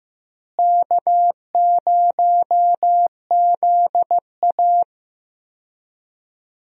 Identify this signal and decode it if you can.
Morse code, heard as a keyed tone, about 15 wpm, 700 Hz. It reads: K0ZA
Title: USA Call Signs - 4 Characters - Repeated 2x - 15wpm